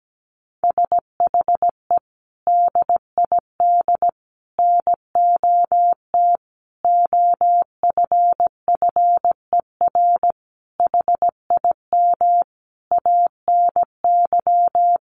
Morse code, heard as a keyed tone, about 17 wpm, 700 Hz. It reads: SHE DID NOT OFFER HIM ANY
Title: Sentences from Top 1500 Words 17wpm